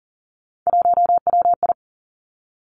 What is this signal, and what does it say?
Morse code, heard as a keyed tone, about 41 wpm, 690 Hz. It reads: 1WI
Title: Three Challenging Letter-Number Combinations 40wpm